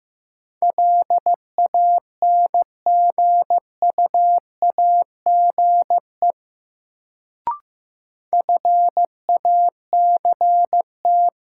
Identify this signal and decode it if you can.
Morse code, heard as a keyed tone, about 15 wpm, 700 Hz. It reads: LANGUAGE FACT
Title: Sentences from Top 500 Words - New Word Drill - 15wpm